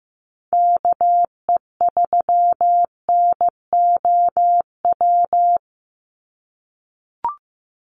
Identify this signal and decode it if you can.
Morse code, heard as a keyed tone, about 15 wpm, 700 Hz. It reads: KE3NOW
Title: USA Call Signs 15wpm